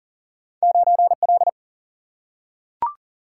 Morse code, heard as a keyed tone, about 40 wpm, 700 Hz. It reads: 9L
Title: Single-Letter Single-Number 40wpm